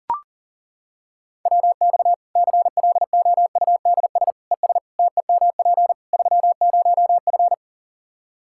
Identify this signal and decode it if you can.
Morse code, heard as a keyed tone, about 37 wpm, 700 Hz. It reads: WX CLOUDS ES TEMP 30F
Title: K1USN SST QSO Practice 40wpm